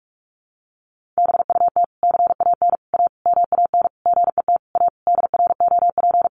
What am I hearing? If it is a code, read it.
Morse code, heard as a keyed tone, about 44 wpm, 700 Hz. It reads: BUT CAN A MAN GET A DROP
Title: Sentences from Top 1400 Words 45wpm